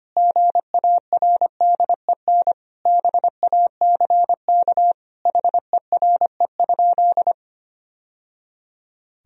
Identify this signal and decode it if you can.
Morse code, heard as a keyed tone, about 25 wpm, 700 Hz. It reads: GARDEN BACK HERE?
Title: Sentences from Top 1000 Words - Review Part 4 - 25wpm